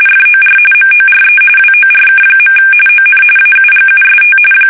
RTTY, about 45.45 baud. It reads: FOR MORE WORK. ON AUGUST
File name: fsk850 / battleshipnj